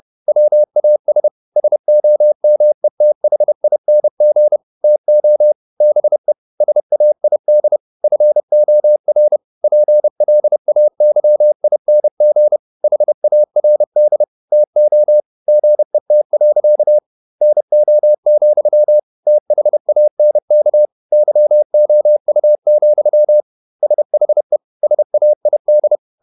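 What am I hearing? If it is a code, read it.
Morse code, heard as a keyed tone, about 29 wpm, 600 Hz. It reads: WAS SOMETHING TO BE SAID FOR PLAYING HARD TO GET. NO, THANK YOU, SHE SAID
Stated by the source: Morse code at 30 wpm, the novel Never by Ken Follett in English, from CW med Gnister.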